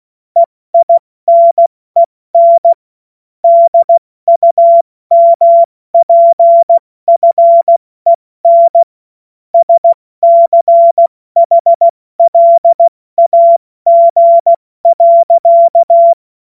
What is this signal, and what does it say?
Morse code, heard as a keyed tone, about 16 wpm, 680 Hz. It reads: EINEN DUMPFEN SCHLAG.
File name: Morsekurs / Witze / Witz 20. September 2022 70 F80